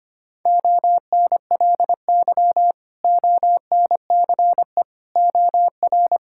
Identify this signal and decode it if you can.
Morse code, heard as a keyed tone, about 25 wpm, 700 Hz. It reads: ONLY ONCE OR
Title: Sentences from Top 1600 Words 25wpm